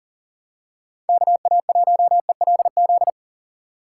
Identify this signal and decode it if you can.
Morse code, heard as a keyed tone, about 40 wpm, 700 Hz. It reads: KA1ELZ
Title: USA Call Signs - 6 Characters - 4x Character Spacing - 40wpm